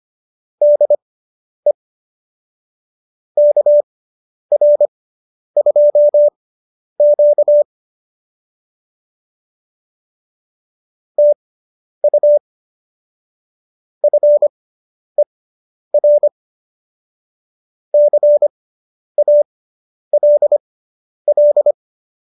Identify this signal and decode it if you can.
Morse code, heard as a keyed tone, about 25 wpm, spaced as for 10 wpm, 590 Hz. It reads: DE KR2Q TU FER CALL